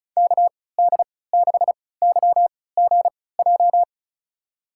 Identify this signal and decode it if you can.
Morse code, heard as a keyed tone, about 36 wpm, spaced as for 20 wpm, 700 Hz. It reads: KD6YGJ
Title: USA Call Signs - 3x Character Spacing - 35wpm